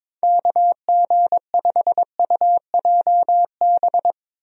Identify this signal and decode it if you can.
Morse code, heard as a keyed tone, about 21 wpm, 700 Hz. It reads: KG5UJB
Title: USA Call Signs - 6 Characters - Repeated 3x - 22wpm